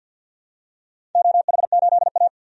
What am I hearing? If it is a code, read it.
Morse code, heard as a keyed tone, about 52 wpm, 700 Hz. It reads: OH8A
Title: DXCC-100 Call Signs - 4 Characters - Repeated 2x - 50wpm